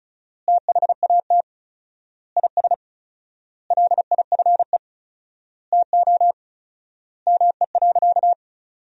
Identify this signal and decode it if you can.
Morse code, heard as a keyed tone, about 34 wpm, 700 Hz. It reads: THAT IS LIFE TO ME.
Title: 4x Word-Spacing - Sentences from Top 500 Words - 35wpm